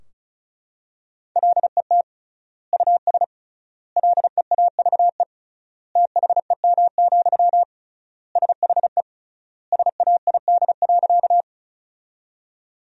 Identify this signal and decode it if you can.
Morse code, heard as a keyed tone, about 36 wpm, 700 Hz. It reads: LET US LEAVE THEM, SHE SAID.
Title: 3x Word-Spacing - Sentences from Top 400 Words - 35wpm